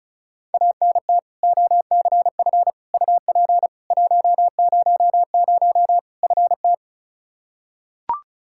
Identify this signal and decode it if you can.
Morse code, heard as a keyed tone, about 34 wpm, 700 Hz. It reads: ANT OCF UP 100 FT
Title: K1USN SST QSO Practice 35wpm